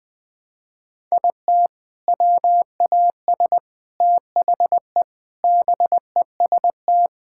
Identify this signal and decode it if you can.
Morse code, heard as a keyed tone, about 20 wpm, 700 Hz. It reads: IT WAS THE BEST
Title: Sets of 8 Words - 20wpm